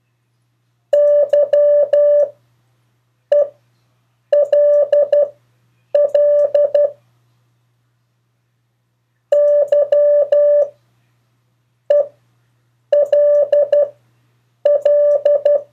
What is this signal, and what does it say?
Morse code, heard as a keyed tone, about 12 wpm, spaced as for 7 wpm, 580 Hz. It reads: YELL YELL
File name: 2of4-words-1